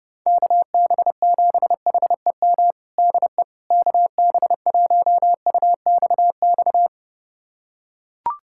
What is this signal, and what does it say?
Morse code, heard as a keyed tone, about 30 wpm, 700 Hz. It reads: KB7HEM DE KB1UXX
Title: USA Callsign DE Callsign - 6 Characters - 30wpm